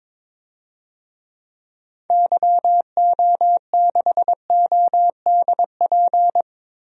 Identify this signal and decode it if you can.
Morse code, heard as a keyed tone, about 22 wpm, 700 Hz. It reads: YO6ODP
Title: DXCC-100 Call Signs - 6 Characters - Repeated 2x - 22wpm